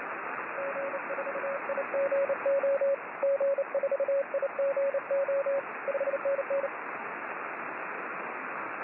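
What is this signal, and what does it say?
Morse code, heard as a keyed tone, about 30 wpm, 550 Hz, with radio noise. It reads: G4IGO G4IGO 5NN
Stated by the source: 6m band Morse from 6O3A.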